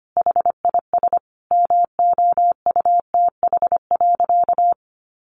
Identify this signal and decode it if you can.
Morse code, heard as a keyed tone, about 25 wpm, 700 Hz. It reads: HIS MOUTH.
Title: Sentences from Top 1500 Words 25wpm